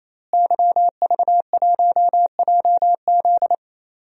Morse code, heard as a keyed tone, about 28 wpm, 700 Hz. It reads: YV1JZ
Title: DXCC-100 Call Signs - 5 Characters - Repeated 3x - 28wpm